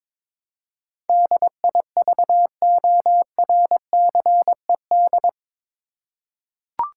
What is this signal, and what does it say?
Morse code, heard as a keyed tone, about 22 wpm, 700 Hz. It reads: DIVORCED
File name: practice / 22wpm / Eight Letter Words - 22wpm